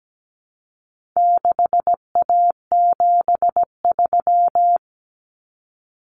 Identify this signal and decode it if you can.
Morse code, heard as a keyed tone, about 17 wpm, 700 Hz. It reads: 6A73
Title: Common Cut Numbers - 4-Digit 17wpm